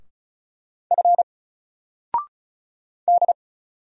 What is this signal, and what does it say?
Morse code, heard as a keyed tone, about 36 wpm, 700 Hz. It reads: F D
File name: Single-Letter-Number-35wpm